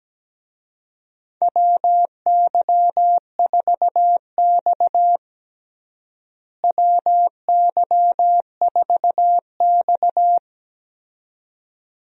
Morse code, heard as a keyed tone, about 17 wpm, 700 Hz. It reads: WY4X WY4X
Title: USA Call Signs - 4 Characters - Repeated 2x - 17wpm